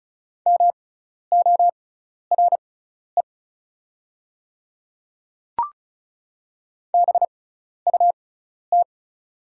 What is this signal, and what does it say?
Morse code, heard as a keyed tone, about 34 wpm, spaced as for 12 wpm, 690 Hz. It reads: MORE BUT
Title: 6x Character Spacing - Top 200 Words - 35wpm